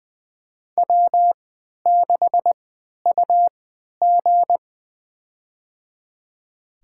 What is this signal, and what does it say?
Morse code, heard as a keyed tone, about 20 wpm, spaced as for 11 wpm, 700 Hz. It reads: W6UG
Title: USA Call Signs - 4 Characters - 3x Character Spacing - 20wpm